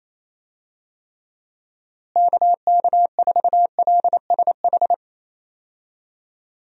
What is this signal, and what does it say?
Morse code, heard as a keyed tone, about 28 wpm, 700 Hz. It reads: KK4LSH
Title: USA Call Signs - 6 Characters - Repeated 3x - 28wpm